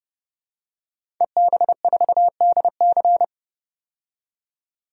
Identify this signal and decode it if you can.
Morse code, heard as a keyed tone, about 30 wpm, 700 Hz. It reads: EB4DC
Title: DXCC-100 Call Signs - 5 Characters - 30wpm